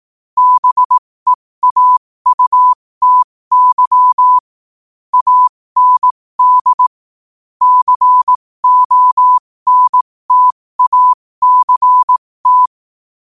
Morse code, heard as a keyed tone, about 18 wpm, spaced as for 15 wpm, 1000 Hz. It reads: BEAUTY AND CONTACT
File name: qotdfiles / 15wpm